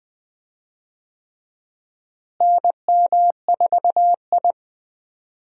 Morse code, heard as a keyed tone, about 20 wpm, 700 Hz. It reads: NM4I
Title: USA Call Signs - 4 Characters - Repeated 3x - 20wpm